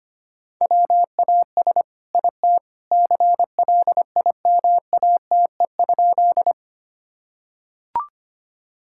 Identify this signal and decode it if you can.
Morse code, heard as a keyed tone, about 25 wpm, 700 Hz. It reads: WAS IT CLIMATE?
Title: Sentences from Top 1600 Words 25wpm